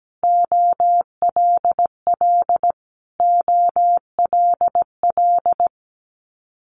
Morse code, heard as a keyed tone, about 17 wpm, 690 Hz. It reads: OLL OLL
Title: Multi-letter Phonemes Repeated 3x 17wpm